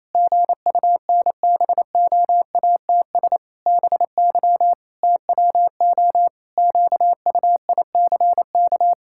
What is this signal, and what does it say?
Morse code, heard as a keyed tone, about 28 wpm, 700 Hz. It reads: GUNBOATS BY TWO QUICK
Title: Pangram Challenge v2 - 28wpm